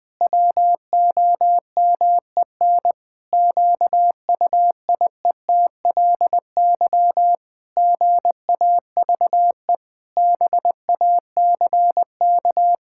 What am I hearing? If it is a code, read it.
Morse code, heard as a keyed tone, about 19 wpm, 700 Hz. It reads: WOMEN QUIETLY GAVE BACK
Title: Pangram Challenge v2 - 20wpm